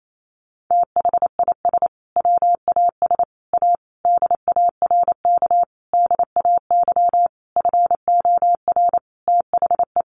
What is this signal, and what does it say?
Morse code, heard as a keyed tone, about 28 wpm, 700 Hz. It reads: THIS WAS A DARK DAY FOR THE
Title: Sentences from Top 1300 Words 28wpm